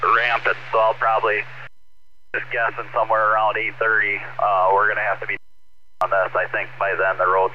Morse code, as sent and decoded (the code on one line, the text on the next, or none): none